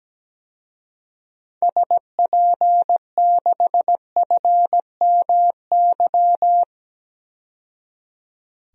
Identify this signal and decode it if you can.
Morse code, heard as a keyed tone, about 17 wpm, 700 Hz. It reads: SP6FMY
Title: DXCC-100 Call Signs - 6 Characters - 17wpm